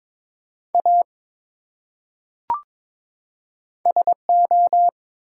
Morse code, heard as a keyed tone, about 23 wpm, 700 Hz. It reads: A SO
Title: Top 180 QSO Elements 22wpm